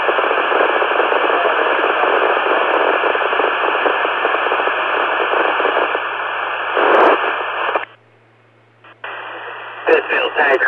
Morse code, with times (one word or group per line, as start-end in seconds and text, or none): none